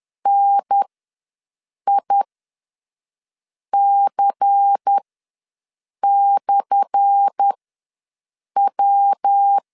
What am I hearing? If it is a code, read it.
Morse code, heard as a keyed tone, about 10 wpm, spaced as for 5 wpm, 800 Hz. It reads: NIC/W
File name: LEZIONI / 10 SIGNS / 10D COMA ws 10wpm 4314